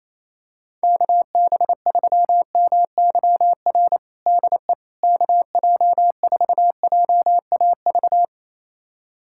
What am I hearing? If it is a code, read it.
Morse code, heard as a keyed tone, about 28 wpm, 700 Hz. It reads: KB3MYR DE KJ4JAV
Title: USA Callsign DE Callsign - 6 Characters - 28wpm